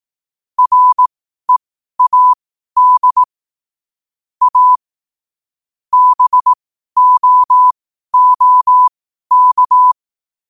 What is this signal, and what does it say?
Morse code, heard as a keyed tone, about 18 wpm, spaced as for 12 wpm, 1000 Hz. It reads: READ A BOOK
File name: qotdfiles / 12wpm